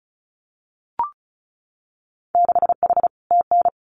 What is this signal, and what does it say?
Morse code, heard as a keyed tone, about 36 wpm, 700 Hz. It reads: 6H TN
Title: Winter Field Day Exchanges Encore 35wpm